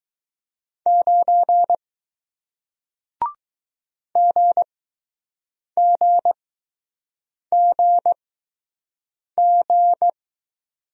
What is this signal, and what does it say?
Morse code, heard as a keyed tone, about 20 wpm, spaced as for 6 wpm, 700 Hz. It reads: 9 GGGG